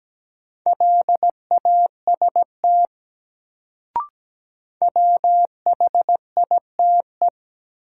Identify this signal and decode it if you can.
Morse code, heard as a keyed tone, about 17 wpm, 700 Hz. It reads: LAST WHITE
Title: Sets of 2 Words - New Word Drill - 17wpm